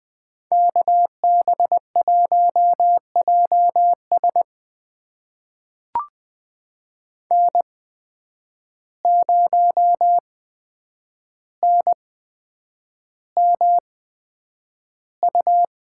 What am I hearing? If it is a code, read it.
Morse code, heard as a keyed tone, about 20 wpm, spaced as for 9 wpm, 700 Hz. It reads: KB1JS N0NMU
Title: USA Call Signs - 5 Characters - 8x Character Spacing - 20wpm